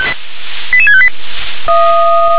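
Five-tone selective call (ZVEI), heard as quick runs of five tones, tones of about 70 ms. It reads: 88658
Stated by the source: ZVEI five-tone call received by SDR.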